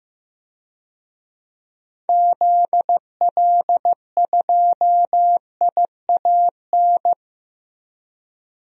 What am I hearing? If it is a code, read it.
Morse code, heard as a keyed tone, about 15 wpm, 700 Hz. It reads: ZL2IAN
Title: DXCC-100 Call Signs - 6 Characters - Repeated 3x - 15wpm